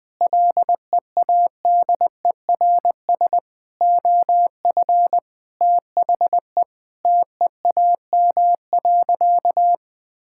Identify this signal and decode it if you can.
Morse code, heard as a keyed tone, about 20 wpm, 700 Hz. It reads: LEADERS OF THE TEAM.
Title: Sentences with 6-Word Phrases - 20wpm